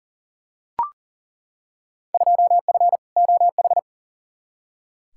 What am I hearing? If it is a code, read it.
Morse code, heard as a keyed tone, about 40 wpm, 700 Hz. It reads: 2F OH
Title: Field Day Exchanges - 40wpm